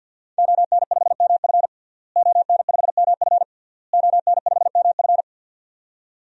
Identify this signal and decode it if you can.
Morse code, heard as a keyed tone, about 48 wpm, 700 Hz. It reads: ON5MF ON5MF ON5MF
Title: DXCC-100 Call Signs - 5 Characters - Repeated 3x - 50wpm